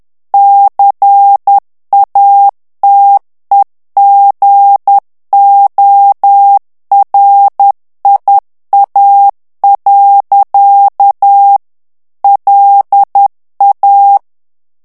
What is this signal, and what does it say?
Morse code, heard as a keyed tone, about 10 wpm, 800 Hz. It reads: CATEGORIA. LA
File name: ITALIEN texte clair 2 vitesse 10 mots minute